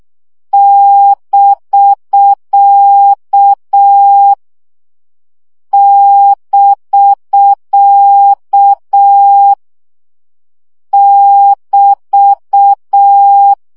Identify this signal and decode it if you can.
Morse code, heard as a keyed tone, about 6 wpm, 800 Hz. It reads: <BK> <BK> <BT>
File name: CW gyakorlas bk 5 baud